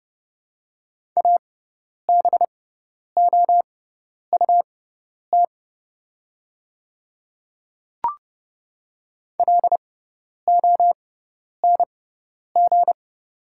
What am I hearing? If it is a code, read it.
Morse code, heard as a keyed tone, about 29 wpm, spaced as for 10 wpm, 690 Hz. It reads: ABOUT LONG
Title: 6x Character Spacing - Top 100 Words - 30wpm